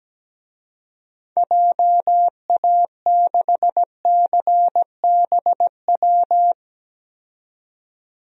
Morse code, heard as a keyed tone, about 17 wpm, 700 Hz. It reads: JA6CBW
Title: DXCC-100 Call Signs - 6 Characters - 17wpm